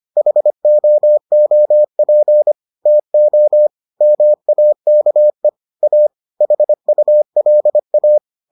Morse code, heard as a keyed tone, about 25 wpm, 600 Hz. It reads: HOOP TO MAKE A HULA